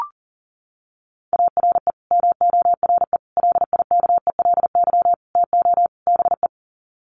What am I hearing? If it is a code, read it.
Morse code, heard as a keyed tone, about 40 wpm, 700 Hz. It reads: ARE MORE LIKELY TO BE
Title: Sets of 5 Words 40wpm